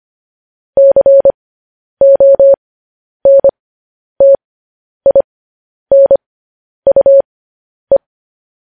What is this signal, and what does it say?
Morse code, heard as a keyed tone, about 25 wpm, spaced as for 10 wpm, 550 Hz. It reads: CONTINUE
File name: podcast / News headlines-1x-25x10